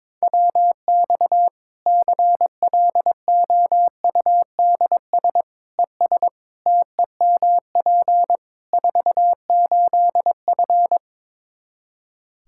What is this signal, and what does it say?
Morse code, heard as a keyed tone, about 22 wpm, 700 Hz. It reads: WX CLOUDS ES TEMP 48F
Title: Practice QSO - Exchange 2 - Info - 22wpm